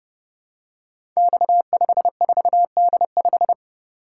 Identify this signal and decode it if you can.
Morse code, heard as a keyed tone, about 30 wpm, 700 Hz. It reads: X54D5
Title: Code Groups - 5 Letter:Numbers - 30wpm